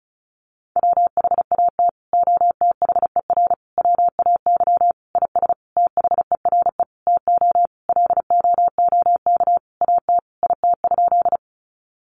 Morse code, heard as a keyed tone, about 34 wpm, 700 Hz. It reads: WHAT OTHER WAY IS THERE TO LOOK AT IT?
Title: Sentences from Top 100 Words 35wpm